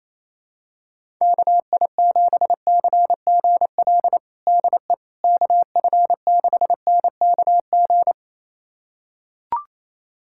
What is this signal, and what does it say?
Morse code, heard as a keyed tone, about 28 wpm, 700 Hz. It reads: KI7CGL DE KF6NKG
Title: USA Callsign DE Callsign - 6 Characters - 28wpm